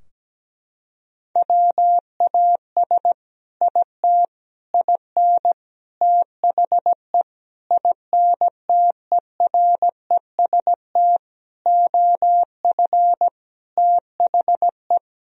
Morse code, heard as a keyed tone, about 17 wpm, 700 Hz. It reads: WAS IT IN THE INTEREST OF THE
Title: Sentences from Top 1000 Words - Review Part 4 - 17wpm